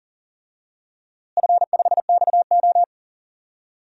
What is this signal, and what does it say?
Morse code, heard as a keyed tone, about 39 wpm, 690 Hz. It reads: F5XO
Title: DXCC-100 Call Signs - 4 Characters - Repeated 2x - 40wpm